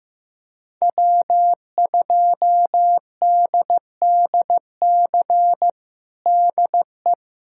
Morse code, heard as a keyed tone, about 16 wpm, 700 Hz. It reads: W2DDC DE
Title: USA Callsign DE Callsign - 5 Characters - 15wpm